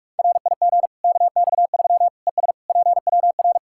Morse code, heard as a keyed tone, about 41 wpm, 700 Hz. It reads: RIG KX3 ES PWR
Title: Practice QSO - Exchange 2 - Info - 45wpm